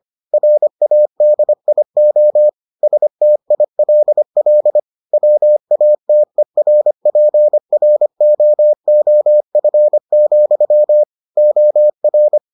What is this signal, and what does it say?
Morse code, heard as a keyed tone, about 25 wpm, 600 Hz. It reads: RADIO STILL WATERPROOF, OR